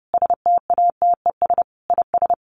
Morse code, heard as a keyed tone, about 30 wpm, 700 Hz. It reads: STATES IS
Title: Sets of 5 Words Encore 30wpm